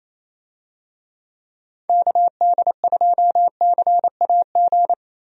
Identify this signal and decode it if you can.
Morse code, heard as a keyed tone, about 27 wpm, 700 Hz. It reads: KD2CAG
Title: USA Call Signs - 6 Characters - 28wpm